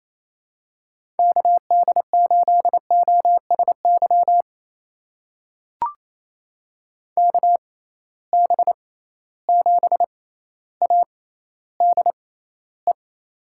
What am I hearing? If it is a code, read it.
Morse code, heard as a keyed tone, about 28 wpm, spaced as for 14 wpm, 700 Hz. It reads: KD8OSY KB7ADE
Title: USA Call Signs - 6 Characters - 6x Character Spacing - 28wpm